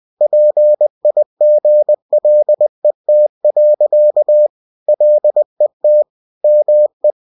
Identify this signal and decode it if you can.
Morse code, heard as a keyed tone, about 20 wpm, 600 Hz. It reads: PIGLET. LET ME
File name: cw-books / winnie-the-pooh / winnie-20wpm-page-0072